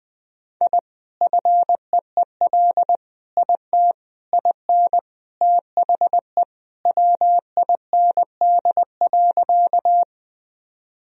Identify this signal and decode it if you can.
Morse code, heard as a keyed tone, about 21 wpm, 700 Hz. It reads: I FEEL IT IN THE WIND.
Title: Sentences from Top 1000 Words - Review Part 2 - 20wpm